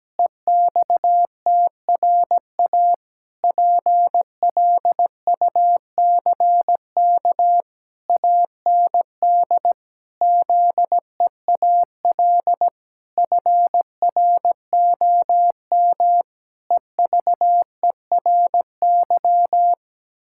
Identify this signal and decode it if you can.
Morse code, heard as a keyed tone, about 17 wpm, 700 Hz. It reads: EXTRA PLUCK AND ZEAL FROM EVERY
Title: Pangram Challenge v2 - 17wpm